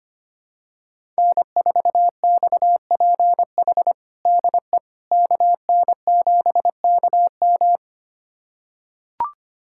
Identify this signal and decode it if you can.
Morse code, heard as a keyed tone, about 25 wpm, 700 Hz. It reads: N4XPH DE KN7KM
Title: USA Callsign DE Callsign - 5 Characters - 25wpm